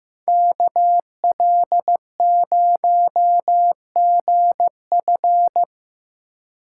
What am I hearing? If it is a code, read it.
Morse code, heard as a keyed tone, about 18 wpm, 700 Hz. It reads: KL0GF
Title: Speed Racing - USA 2x2 Call Signs 15wpm